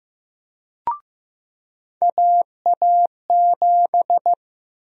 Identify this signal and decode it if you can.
Morse code, heard as a keyed tone, about 15 wpm, 700 Hz. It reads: AA7
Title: USA Call Sign Prefixes - Repeated 2x - 15wpm